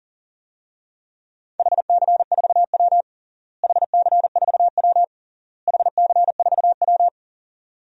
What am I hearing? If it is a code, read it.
Morse code, heard as a keyed tone, about 40 wpm, 700 Hz. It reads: HC4W HC4W HC4W
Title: DXCC-100 Call Signs - 4 Characters - Repeated 3x - 40wpm